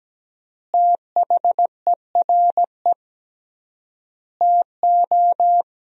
Morse code, heard as a keyed tone, about 17 wpm, 700 Hz. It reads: THERE TO
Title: 3x Word-Spacing - Sentences from Top 100 Words - 17wpm